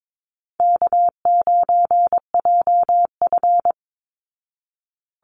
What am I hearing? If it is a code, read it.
Morse code, heard as a keyed tone, about 22 wpm, 700 Hz. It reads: K9JF
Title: Easy Call Signs Repeated 2x - 22wpm